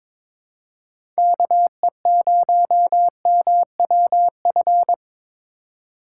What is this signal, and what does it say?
Morse code, heard as a keyed tone, about 22 wpm, 700 Hz. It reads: KE0MWF
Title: USA Call Signs - 6 Characters - 2x Character Spacing - 22wpm